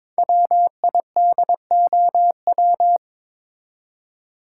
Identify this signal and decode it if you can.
Morse code, heard as a keyed tone, about 22 wpm, 690 Hz. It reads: WIDOW
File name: 5 Letter Words - First 3 Letters Given - 22wpm